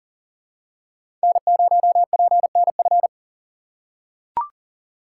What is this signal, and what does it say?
Morse code, heard as a keyed tone, about 40 wpm, 700 Hz. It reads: N0PNF
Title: USA Call Signs - 5 Characters - 6x Character Spacing - 40wpm